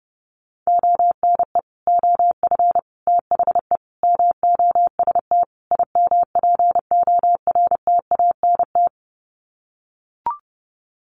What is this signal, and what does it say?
Morse code, heard as a keyed tone, about 30 wpm, 700 Hz. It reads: ONE OF THE MOST IMPORTANT
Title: Sets of 5 Words 30wpm